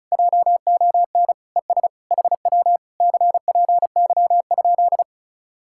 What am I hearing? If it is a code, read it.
Morse code, heard as a keyed tone, about 40 wpm, 700 Hz. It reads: JON ES HW CPY?